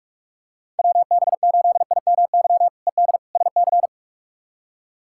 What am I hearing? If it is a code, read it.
Morse code, heard as a keyed tone, about 46 wpm, 700 Hz. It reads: WB8IMY ED SC
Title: K1USN SST QSO Practice - 45wpm